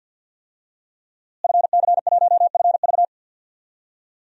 Morse code, heard as a keyed tone, about 46 wpm, 700 Hz. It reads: FC1FV
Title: DXCC-100 Call Signs - 5 Characters - Repeated 3x - 50wpm